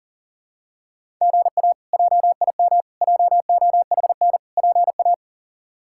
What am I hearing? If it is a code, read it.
Morse code, heard as a keyed tone, about 40 wpm, 700 Hz. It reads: GA JIM JOHN PA
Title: K1USN SST QSO Practice - 40wpm